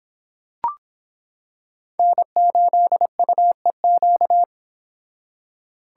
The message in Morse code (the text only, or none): N8UEQ